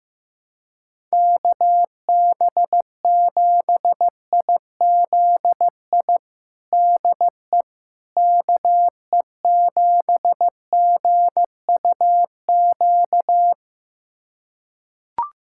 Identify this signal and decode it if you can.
Morse code, heard as a keyed tone, about 15 wpm, 700 Hz. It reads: KB7IZI DE KE7GUQ
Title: USA Callsign DE Callsign - 6 Characters - 15wpm